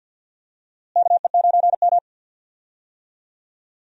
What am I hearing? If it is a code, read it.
Morse code, heard as a keyed tone, about 52 wpm, 700 Hz. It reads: KE9M
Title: USA Call Signs - 4 Characters - Repeated 3x - 50wpm